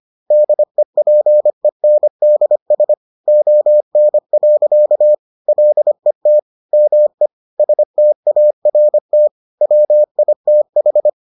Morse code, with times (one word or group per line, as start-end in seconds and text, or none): 0.3-2.9 DEPENDS
3.3-5.1 ON.
5.5-6.4 LET
6.7-7.3 ME
7.6-9.3 START
9.6-11.1 WITH